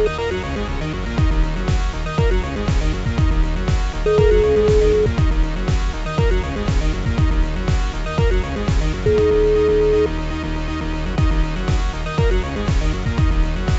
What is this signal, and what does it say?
Ringback tone (Europe, 425 Hz), with noise.